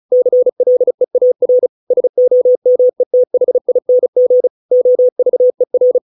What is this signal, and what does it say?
Morse code, heard as a keyed tone, about 34 wpm, 500 Hz. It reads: CLEAR SOMETHING OVER